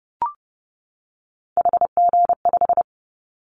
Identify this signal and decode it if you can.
Morse code, heard as a keyed tone, about 30 wpm, 700 Hz. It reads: HG5
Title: Callsign Prefixes for Top 50 DXCC - 30wpm